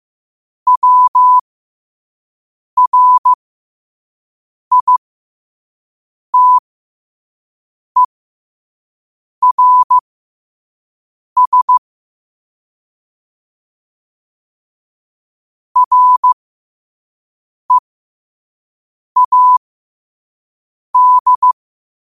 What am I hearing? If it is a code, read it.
Morse code, heard as a keyed tone, about 15 wpm, spaced as for 5 wpm, 1000 Hz. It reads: WRITERS READ
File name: qotdfiles / 5wpm